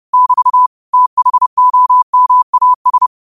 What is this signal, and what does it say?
Morse code, heard as a keyed tone, about 30 wpm, 1000 Hz. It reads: <BT> THOMAS
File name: 30wpm